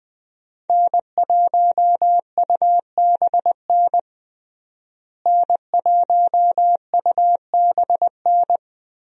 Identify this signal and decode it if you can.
Morse code, heard as a keyed tone, about 21 wpm, 700 Hz. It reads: N1UBN N1UBN
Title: USA Call Signs - 5 Characters - Repeated 2x - 20wpm